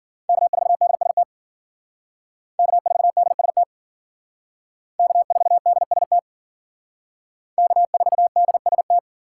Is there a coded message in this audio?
Yes: Morse code from 0.3 to 9.0 s, heard as a keyed tone, about 51 wpm, 700 Hz. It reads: K4DST K4DST K4DST K4DST